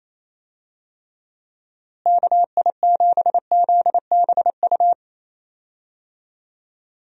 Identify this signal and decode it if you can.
Morse code, heard as a keyed tone, about 28 wpm, 700 Hz. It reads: KI7ZBU
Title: USA Call Signs - 6 Characters - Repeated 3x - 28wpm